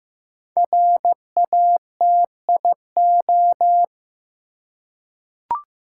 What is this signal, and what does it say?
Morse code, heard as a keyed tone, about 15 wpm, 700 Hz. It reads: RATIO
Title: Five Letter Words - 15wpm